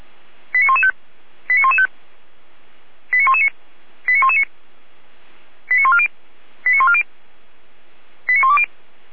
Five-tone selective call (ZVEI), heard as quick runs of five tones, tones of about 70 ms. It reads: 89115 89115 89119 89119 89140 89140 89120